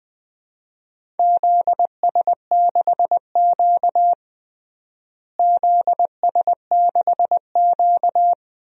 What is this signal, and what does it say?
Morse code, heard as a keyed tone, about 20 wpm, 700 Hz. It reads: ZS6Q ZS6Q
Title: DXCC-100 Call Signs - 4 Characters - Repeated 2x - 20wpm